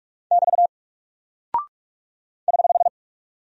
Morse code, heard as a keyed tone, about 46 wpm, 700 Hz. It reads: <BT> <HH>
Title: Single Character (Letter:Number:Punctuation:Prosign) 45wpm